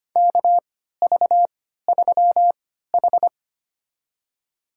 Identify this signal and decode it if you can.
Morse code, heard as a keyed tone, about 25 wpm, spaced as for 14 wpm, 700 Hz. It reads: KV3H
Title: USA Call Signs - 3x Character Spacing - 25wpm